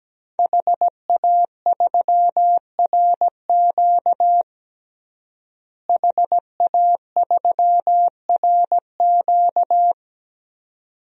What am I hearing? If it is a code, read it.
Morse code, heard as a keyed tone, about 17 wpm, 700 Hz. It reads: HA3RQ HA3RQ
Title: DXCC-100 Call Signs - 5 Characters - Repeated 2x - 17wpm